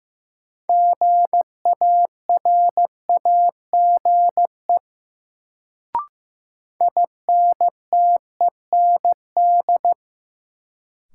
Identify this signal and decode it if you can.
Morse code, heard as a keyed tone, about 15 wpm, 700 Hz. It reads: GARAGE INTEND
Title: Sentences from Top 1600 Words - New Word Drill - 15wpm